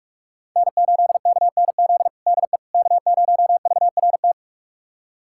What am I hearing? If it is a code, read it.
Morse code, heard as a keyed tone, about 44 wpm, 700 Hz. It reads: N8KNZ DE K0VRT
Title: USA Callsign DE Callsign - 5 Characters - 45wpm